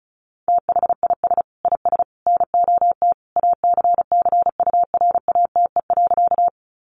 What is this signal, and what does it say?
Morse code, heard as a keyed tone, about 34 wpm, 700 Hz. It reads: THIS IS NOT ACCURATE.
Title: Sentences from Top 1200 Words 35wpm